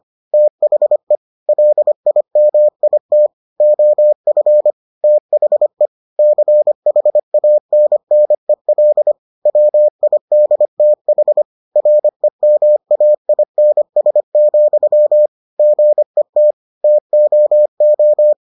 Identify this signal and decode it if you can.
Morse code, heard as a keyed tone, about 25 wpm, 600 Hz. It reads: THE LIMIT OF THE CHANNEL WIDTH REMAINS, GET TOO